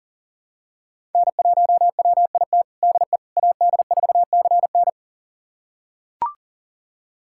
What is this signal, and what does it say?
Morse code, heard as a keyed tone, about 40 wpm, 700 Hz. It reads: N1WIT DE AD4CN
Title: USA Callsign DE Callsign - 5 Characters - 40wpm